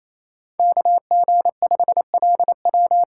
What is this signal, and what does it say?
Morse code, heard as a keyed tone, about 27 wpm, 700 Hz. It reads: KG5LW
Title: USA Call Signs - 6x Character Spacing - 28wpm